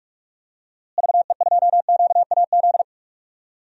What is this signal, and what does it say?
Morse code, heard as a keyed tone, about 43 wpm, 700 Hz. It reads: VE2QAZ
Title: DXCC-100 Call Signs - 6 Characters - Repeated 2x - 45wpm